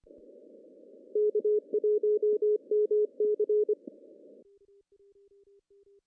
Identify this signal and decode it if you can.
Morse code, heard as a keyed tone, about 25 wpm, 420 Hz. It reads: K1MC
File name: CW / cqww-cw-k5tr